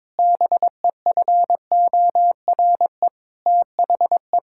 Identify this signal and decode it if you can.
Morse code, heard as a keyed tone, about 22 wpm, 700 Hz. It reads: BEFORE THE
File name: Sentences with 9-Word Phrases - 22wpm